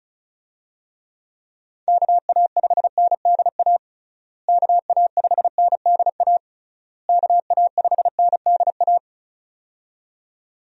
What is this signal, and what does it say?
Morse code, heard as a keyed tone, about 36 wpm, 700 Hz. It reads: KA5NDA KA5NDA KA5NDA
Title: USA Call Signs - 6 Characters - Repeated 3x - 35wpm